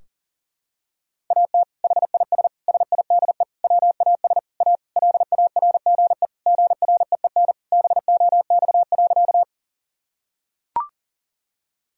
Morse code, heard as a keyed tone, about 40 wpm, 700 Hz. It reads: AT HIS SIDE WAS A LARGE GREEN BOX.
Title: Sentences from Top 1000 Words - Review Part 2 - 40wpm